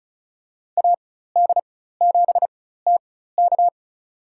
Morse code, heard as a keyed tone, about 36 wpm, spaced as for 16 wpm, 700 Hz. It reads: AD7TK
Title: USA Call Signs - 5 Characters - 4x Character Spacing - 35wpm